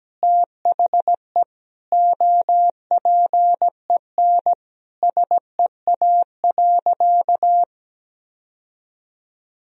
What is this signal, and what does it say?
Morse code, heard as a keyed tone, about 17 wpm, 700 Hz. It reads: THE OPEN SEA.
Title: Sentences from Top 1000 Words - Review Part 1 - 17wpm